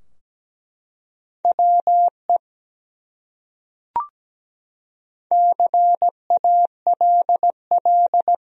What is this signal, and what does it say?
Morse code, heard as a keyed tone, about 17 wpm, 700 Hz. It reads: WE CALL
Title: Top 100 Words Repeated 17wpm